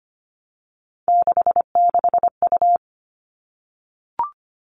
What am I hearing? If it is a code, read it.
Morse code, heard as a keyed tone, about 25 wpm, 700 Hz. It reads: 66U
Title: All Cut Numbers - 3-Digit 25wpm